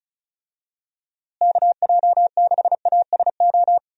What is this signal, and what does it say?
Morse code, heard as a keyed tone, about 36 wpm, 700 Hz. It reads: KJ6ASO
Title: USA Call Signs - 6 Characters - Repeated 3x - 35wpm